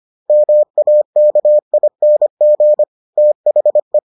Morse code, heard as a keyed tone, about 25 wpm, 600 Hz. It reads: MAKING THE